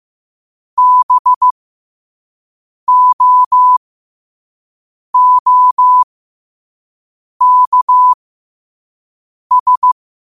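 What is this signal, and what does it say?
Morse code, heard as a keyed tone, about 15 wpm, spaced as for 5 wpm, 1000 Hz. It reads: BOOKS
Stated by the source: Morse code sent at 5 wpm.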